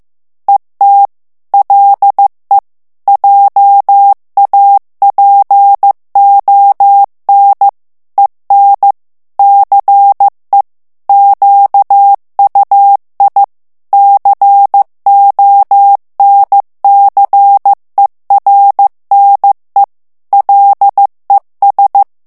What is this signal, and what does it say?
Morse code, heard as a keyed tone, about 15 wpm, 800 Hz. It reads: ET LE JAPON EN CE QUI CONCERNE LES
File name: morse mp3 / FRANCAIS TEXTE CLAIR 1 VITESSE 14 MOTS MINUTES